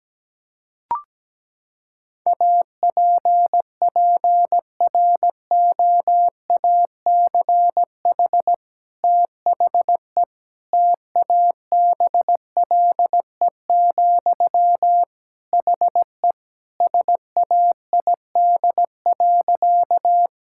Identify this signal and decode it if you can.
Morse code, heard as a keyed tone, about 17 wpm, 700 Hz. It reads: APPROACH THE TABLE, HE SAID.
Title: Sentences from Top 1000 Words - Review Part 6 - 17wpm